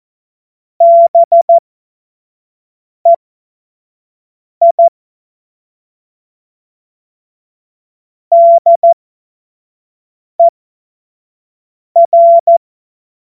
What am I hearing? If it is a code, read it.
Morse code, heard as a keyed tone, about 14 wpm, spaced as for 5 wpm, 680 Hz. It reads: BEI DER
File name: Witz-16-September-2025-25-F70